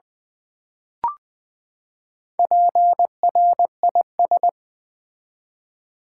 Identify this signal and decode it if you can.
Morse code, heard as a keyed tone, about 21 wpm, 700 Hz. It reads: PRIS
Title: Build A Word - 4 Letters At A Time - 8-Letter Words - 20wpm